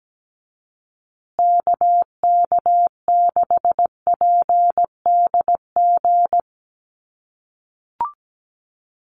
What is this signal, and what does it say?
Morse code, heard as a keyed tone, about 17 wpm, 700 Hz. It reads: KK6PDG
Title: USA Call Signs Repeated 3x 17wpm